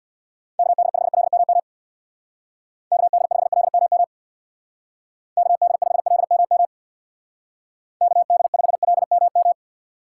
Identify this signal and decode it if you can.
Morse code, heard as a keyed tone, about 62 wpm, 700 Hz. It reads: KD5LMM KD5LMM KD5LMM KD5LMM
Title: Speed Racing - USA 2x3 Call Signs 50wpm